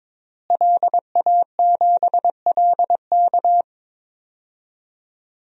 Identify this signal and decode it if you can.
Morse code, heard as a keyed tone, about 22 wpm, 700 Hz. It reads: LA7LK
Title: DXCC-100 Call Signs - 5 Characters - Repeated 2x - 22wpm